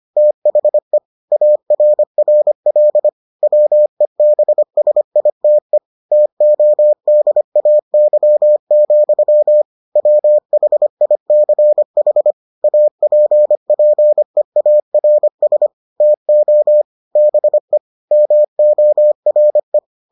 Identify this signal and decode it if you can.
Morse code, heard as a keyed tone, about 25 wpm, 600 Hz. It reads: THE ARRL WEBSITE TODAY, WHICH APPEARS TO BE MORE